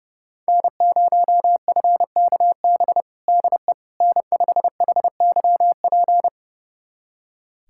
Morse code, heard as a keyed tone, about 31 wpm, 700 Hz. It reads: N0FKB DE N5HYP
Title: USA Callsign DE Callsign - 5 Characters - 30wpm